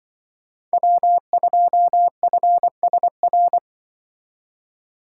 Morse code, heard as a keyed tone, about 25 wpm, 700 Hz. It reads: W2FSR
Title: Speed Racing - USA Call Signs 20wpm